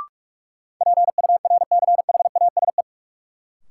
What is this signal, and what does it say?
Morse code, heard as a keyed tone, about 46 wpm, 700 Hz. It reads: PURCHASE
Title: Sentences from Top 800 Words - New Word Drill - 45wpm